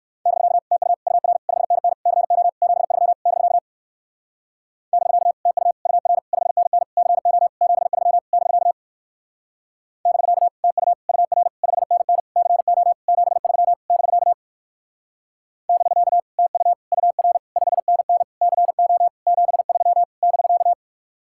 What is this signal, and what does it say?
Morse code, heard as a keyed tone, about 55 wpm, 700 Hz. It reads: <BK> TU UR 5NN CO 73 <BK> <BK> TU UR 5NN CO 73 <BK> <BK> TU UR 5NN CO 73 <BK> <BK> TU UR 5NN CO 73 <BK>
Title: Speed-Racing Common QSO Phrases - 45wpm